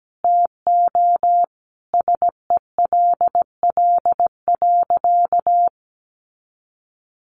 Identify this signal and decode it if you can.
Morse code, heard as a keyed tone, about 17 wpm, 700 Hz. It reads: TO SELL.
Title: Sentences from Top 1100 Words 17wpm